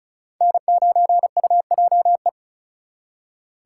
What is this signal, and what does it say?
Morse code, heard as a keyed tone, about 39 wpm, 700 Hz. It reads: N9UJE
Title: Speed Racing - USA 1x3 Call Signs 35wpm